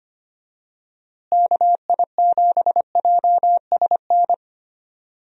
Morse code, heard as a keyed tone, about 25 wpm, 700 Hz. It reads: KI7JSN
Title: USA Call Signs - 6 Characters - Repeated 3x - 25wpm